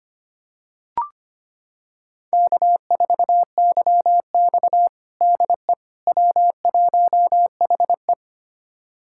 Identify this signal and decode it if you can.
Morse code, heard as a keyed tone, about 25 wpm, 700 Hz. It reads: K4YX DE W1HE
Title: USA Callsign DE Callsign - 4 Characters - 25wpm